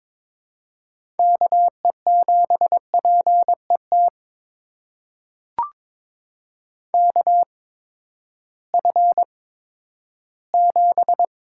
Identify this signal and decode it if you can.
Morse code, heard as a keyed tone, about 22 wpm, spaced as for 12 wpm, 700 Hz. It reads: KE7PET KF7
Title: USA Call Signs - 6 Characters - 8x Character Spacing - 22wpm